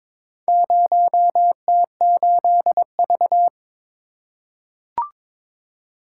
Morse code, heard as a keyed tone, about 22 wpm, 700 Hz. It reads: 0T8V
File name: practice / 22wpm / Four Letter-Number Combinations - 22wpm